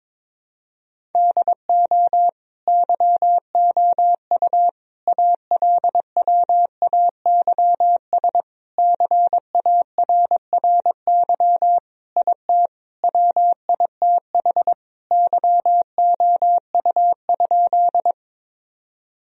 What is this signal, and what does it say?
Morse code, heard as a keyed tone, about 22 wpm, 700 Hz. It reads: DO YOU ALWAYS CARRY IT WITH YOU?
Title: Sentences from Top 1000 Words - Review Part 1 - 22wpm